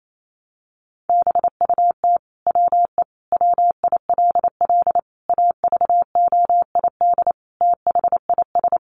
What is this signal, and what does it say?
Morse code, heard as a keyed tone, about 28 wpm, 700 Hz. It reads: BUT WE WILL AVOID THIS
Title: Sentences from Top 1200 Words 28wpm